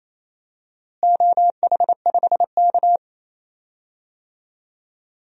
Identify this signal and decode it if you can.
Morse code, heard as a keyed tone, about 28 wpm, 700 Hz. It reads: OH5K
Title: DXCC-100 Call Signs - 4 Characters - Repeated 2x - 28wpm